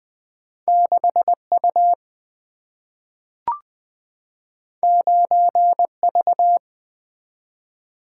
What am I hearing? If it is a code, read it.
Morse code, heard as a keyed tone, about 20 wpm, 700 Hz. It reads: 6 U 9 V
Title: Single-Letter Single-Number 20wpm